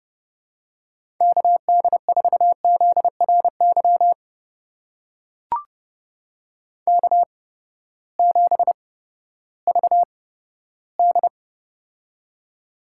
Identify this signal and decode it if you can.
Morse code, heard as a keyed tone, about 31 wpm, spaced as for 15 wpm, 700 Hz. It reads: KD4ZRY K7VD
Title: USA Call Signs - 8x Character Spacing - 30wpm